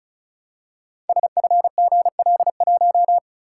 Morse code, heard as a keyed tone, about 36 wpm, 700 Hz. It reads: SFGL1
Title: Code Groups - 5 Letter:Numbers - 35wpm